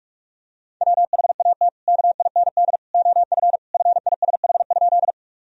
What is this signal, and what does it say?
Morse code, heard as a keyed tone, about 46 wpm, 700 Hz. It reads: WHAT KIND OF FISH?
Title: Sentences from Top 1000 Words - Review Part 1 - 45wpm